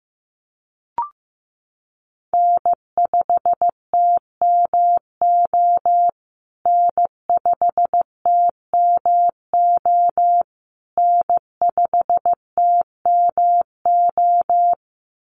Morse code, heard as a keyed tone, about 15 wpm, 700 Hz. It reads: N5TMO N5TMO N5TMO
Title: USA Call Signs Repeated 3x 15wpm